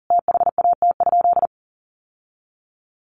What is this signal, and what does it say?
Morse code, heard as a keyed tone, about 40 wpm, 700 Hz. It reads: THAT?
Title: Sentences from Top 600 Words 40wpm